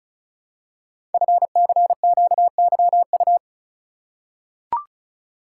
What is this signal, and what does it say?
Morse code, heard as a keyed tone, about 36 wpm, 700 Hz. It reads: FCQYU
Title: Code Groups - 5 Letters - 35wpm